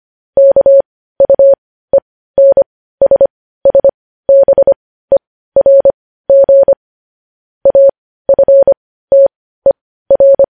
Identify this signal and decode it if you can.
Morse code, heard as a keyed tone, about 25 wpm, spaced as for 15 wpm, 550 Hz. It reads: KUENSSBERG AFTER
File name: podcast / News headlines-1x-25x15